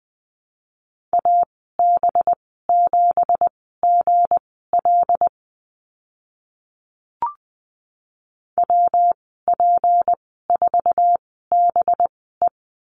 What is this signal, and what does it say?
Morse code, heard as a keyed tone, about 20 wpm, spaced as for 14 wpm, 700 Hz. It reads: AB7GL WP4BE
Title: 193 Lesson 31g - Letter v - Call Signs 20wpm